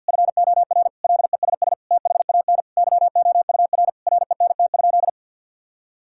Morse code, heard as a keyed tone, about 48 wpm, 700 Hz. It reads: FOR LESS THAN YOUR RENT?
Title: Sentences from Top 1000 Words - Review Part 5 - 50wpm